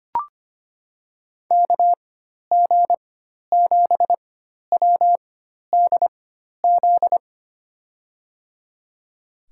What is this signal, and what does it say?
Morse code, heard as a keyed tone, about 25 wpm, spaced as for 12 wpm, 700 Hz. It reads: KG7WDZ
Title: USA Call Signs - 6 Characters - 4x Character Spacing - 25wpm